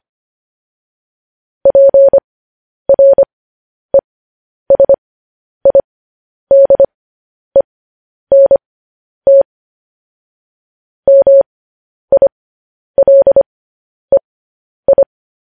Morse code, heard as a keyed tone, about 25 wpm, spaced as for 10 wpm, 550 Hz. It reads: PRESIDENT MILEI
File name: podcast / News headlines-1x-25x10